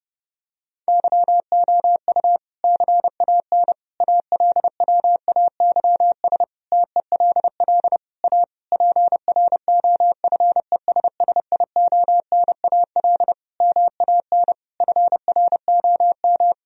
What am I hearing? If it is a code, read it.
Morse code, heard as a keyed tone, about 32 wpm, 700 Hz. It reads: YOU CAN ALWAYS TELL A PROFESSIONAL MAN FROM
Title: Sentences from Top 1000 Words - Review Part 4 - 30wpm